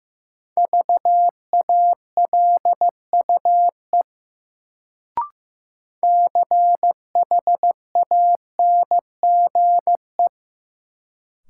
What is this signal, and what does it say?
Morse code, heard as a keyed tone, about 16 wpm, 700 Hz. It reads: VALUE CHANGE
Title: Sentences from Top 600 Words - New Word Drill - 15wpm